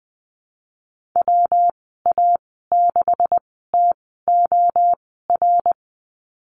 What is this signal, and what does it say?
Morse code, heard as a keyed tone, about 19 wpm, spaced as for 14 wpm, 700 Hz. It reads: WA6TOR
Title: 128 Lesson 21g - Number 6 - Call Signs 20wpm